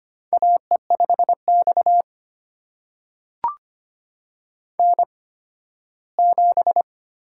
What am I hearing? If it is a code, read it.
Morse code, heard as a keyed tone, about 25 wpm, spaced as for 15 wpm, 700 Hz. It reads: AE5X N7
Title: USA Call Signs - 4 Characters - 8x Character Spacing - 25wpm